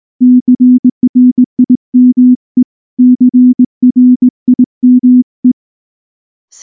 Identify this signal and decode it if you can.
Morse code, heard as a keyed tone, about 19 wpm, 250 Hz. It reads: CRIME CRIME
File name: morse / morse-training-20WPM-10